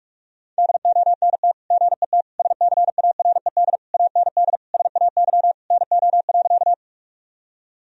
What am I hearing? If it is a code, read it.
Morse code, heard as a keyed tone, about 46 wpm, 700 Hz. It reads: DONT GET SCARED AND SAY NO.
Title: Sentences from Top 1000 Words - Review Part 4 - 45wpm